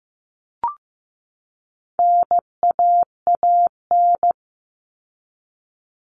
Morse code, heard as a keyed tone, about 15 wpm, 700 Hz. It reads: NAAN
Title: All Cut Numbers - 4-Digit 15wpm